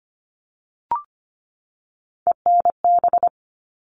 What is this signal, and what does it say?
Morse code, heard as a keyed tone, about 25 wpm, 700 Hz. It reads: ENB
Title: All Cut Numbers - 3-Digit 25wpm